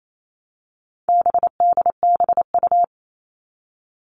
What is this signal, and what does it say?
Morse code, heard as a keyed tone, about 28 wpm, 700 Hz. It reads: BDBU
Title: All Cut Numbers - 4-Digit 28wpm